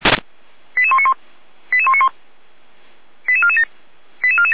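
Five-tone selective call (ZVEI), heard as quick runs of five tones, tones of about 70 ms. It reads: 88191 88191 88447 88447
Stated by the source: ZVEI five-tone selective call received by SDR.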